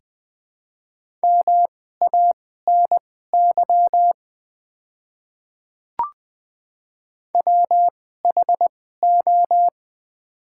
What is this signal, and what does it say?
Morse code, heard as a keyed tone, about 21 wpm, spaced as for 15 wpm, 700 Hz. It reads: MANY WHO
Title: 2x Character Spacing - Top 100 Words - 20wpm